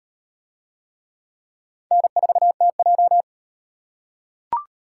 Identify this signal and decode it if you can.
Morse code, heard as a keyed tone, about 38 wpm, 700 Hz. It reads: N4TJ